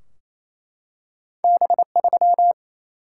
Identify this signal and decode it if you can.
Morse code, heard as a keyed tone, about 28 wpm, 700 Hz. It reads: B3
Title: Single-Letter Single-Number 28wpm